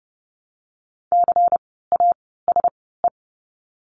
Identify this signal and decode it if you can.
Morse code, heard as a keyed tone, about 30 wpm, 700 Hz. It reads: C A S E
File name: Lesson 16f - Letter c - Words 30wpm